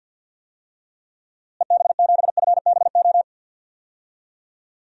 Morse code, heard as a keyed tone, about 53 wpm, 700 Hz. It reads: EB7FBO
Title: DXCC-100 Call Signs - 6 Characters - Repeated 2x - 50wpm